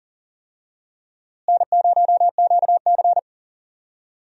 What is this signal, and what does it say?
Morse code, heard as a keyed tone, about 39 wpm, 700 Hz. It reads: N0QC
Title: USA Call Signs - 4 Characters - Repeated 3x - 40wpm